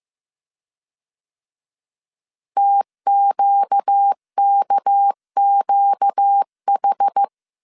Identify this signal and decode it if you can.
Morse code, heard as a keyed tone, about 15 wpm, 800 Hz. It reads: TQKQH
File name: CW / DET / 1 6 1500C S A 14wpm 2921